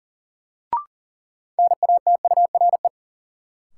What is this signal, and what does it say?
Morse code, heard as a keyed tone, about 40 wpm, 700 Hz. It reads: NATURE
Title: Sets of 3 Words - New Word Drill - 40wpm